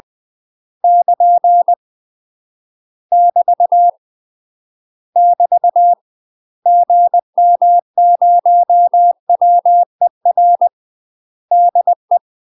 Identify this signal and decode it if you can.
Morse code, heard as a keyed tone, about 20 wpm, 700 Hz. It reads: <KN> <BT> <BT> GM0WER DE
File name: UK MASTER QSOs TRAINING / 20WPM / UK MASTER QSOs 0007